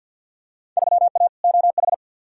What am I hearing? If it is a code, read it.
Morse code, heard as a keyed tone, about 51 wpm, 700 Hz. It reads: 3A OH
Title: Field Day Exchanges - 50wpm